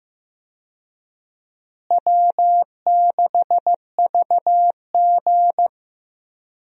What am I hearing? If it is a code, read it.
Morse code, heard as a keyed tone, about 15 wpm, 700 Hz. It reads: W6VG
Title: USA Call Signs - 4 Characters - Repeated 2x - 15wpm